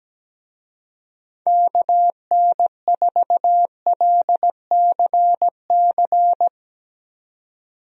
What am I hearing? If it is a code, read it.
Morse code, heard as a keyed tone, about 17 wpm, 700 Hz. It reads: KN4LCC
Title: USA Call Signs - 6 Characters - 17wpm